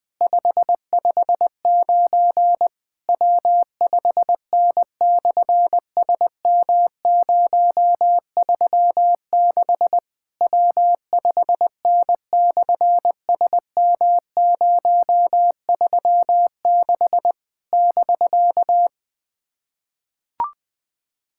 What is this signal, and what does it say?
Morse code, heard as a keyed tone, about 20 wpm, 700 Hz. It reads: 559 W5N/SM036 W5N/SM036 <BK>
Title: SOTA QSO Practice 20wpm